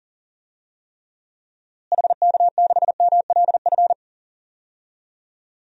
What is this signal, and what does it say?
Morse code, heard as a keyed tone, about 40 wpm, 700 Hz. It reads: HK6MLF
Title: DXCC-100 Call Signs - 6 Characters - Repeated 2x - 40wpm